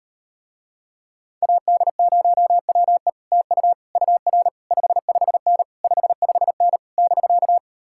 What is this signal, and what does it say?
Morse code, heard as a keyed tone, about 39 wpm, 700 Hz. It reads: AD0WE TU UR 55N 55N <BK>